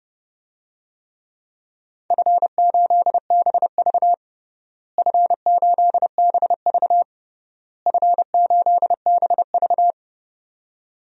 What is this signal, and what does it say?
Morse code, heard as a keyed tone, about 32 wpm, 700 Hz. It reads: F8BV F8BV F8BV
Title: DXCC-100 Call Signs - 4 Characters - Repeated 3x - 30wpm